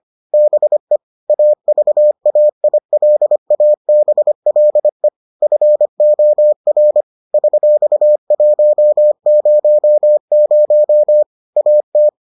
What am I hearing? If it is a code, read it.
Morse code, heard as a keyed tone, about 25 wpm, 600 Hz. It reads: BE AVAILABLE FOR $100 AT